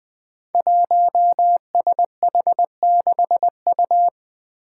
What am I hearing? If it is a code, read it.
Morse code, heard as a keyed tone, about 20 wpm, 700 Hz. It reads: 1SH6U
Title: Five Challenging Letter-Number Combinations 20wpm